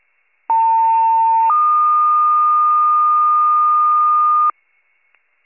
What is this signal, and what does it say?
Two-tone page, heard as a one-second tone, then a three-second tone, about 900 then 1220 Hz. It is heard from 0.5 to 4.5 s.